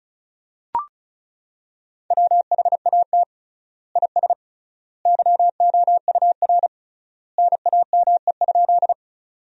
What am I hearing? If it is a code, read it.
Morse code, heard as a keyed tone, about 36 wpm, 700 Hz. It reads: WHAT IS YOUR NAME?
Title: 3x Word-Spacing - Sentences from Top 200 Words - 35wpm